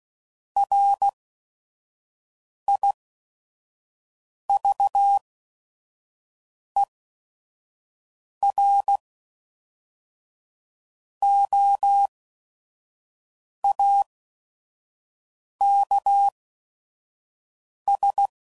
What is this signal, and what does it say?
Morse code, heard as a keyed tone, about 16 wpm, spaced as for 5 wpm, 800 Hz. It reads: RIVEROAKS
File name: CW Code Groups @5WPM / fromSM's / Cities